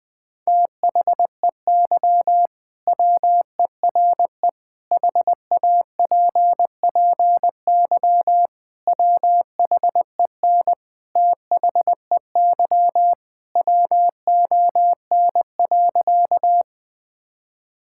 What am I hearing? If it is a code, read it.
Morse code, heard as a keyed tone, about 21 wpm, 700 Hz. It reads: THEY WERE HAPPY WHEN THEY WON.
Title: Sentences with 2-Word Phrases - 20wpm